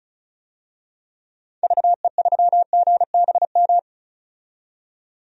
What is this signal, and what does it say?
Morse code, heard as a keyed tone, about 34 wpm, 700 Hz. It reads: VE3GBM
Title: DXCC-100 Call Signs - 6 Characters - 35wpm